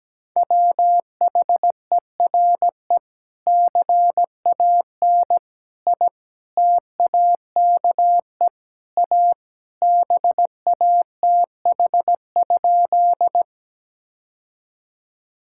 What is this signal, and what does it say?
Morse code, heard as a keyed tone, about 17 wpm, 700 Hz. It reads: WHERE CAN I TAKE A BATH?
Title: Sentences from Top 1600 Words 17wpm